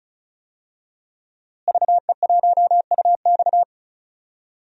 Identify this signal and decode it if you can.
Morse code, heard as a keyed tone, about 34 wpm, 700 Hz. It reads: VE1UX
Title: DXCC-100 Call Signs - 5 Characters - Repeated 3x - 35wpm